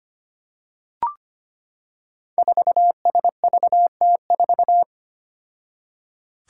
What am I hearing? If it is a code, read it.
Morse code, heard as a keyed tone, about 25 wpm, 700 Hz. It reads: 4SVT4
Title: Five Challenging Letter-Number Combinations 25wpm